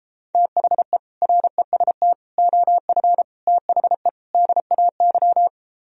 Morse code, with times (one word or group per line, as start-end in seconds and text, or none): 0.3-1.0 THE
1.2-2.1 REST
2.4-3.2 OF
3.5-4.1 THE
4.3-5.5 DAY